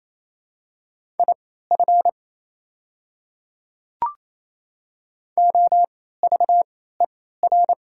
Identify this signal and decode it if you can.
Morse code, heard as a keyed tone, about 28 wpm, spaced as for 16 wpm, 700 Hz. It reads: IF OVER
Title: 3x Character Spacing - Top 200 Words - 28wpm